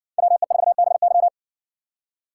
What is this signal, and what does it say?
Morse code, heard as a keyed tone, about 62 wpm, 700 Hz. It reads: WE4BY